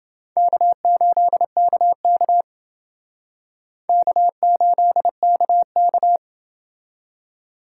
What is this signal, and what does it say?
Morse code, heard as a keyed tone, about 29 wpm, 700 Hz. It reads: K8KK K8KK